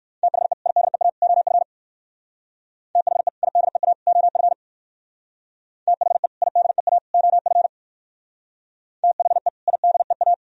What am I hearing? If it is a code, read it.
Morse code, heard as a keyed tone, about 55 wpm, 700 Hz. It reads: THE IDEA OF THE IDEA OF THE IDEA OF THE IDEA
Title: Speed-Racing - Sets of 3 Words 45wpm